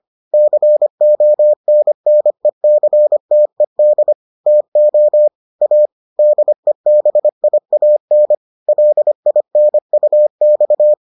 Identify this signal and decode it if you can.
Morse code, heard as a keyed tone, about 25 wpm, 600 Hz. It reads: CONNECTED TO A DEBIAN LINUX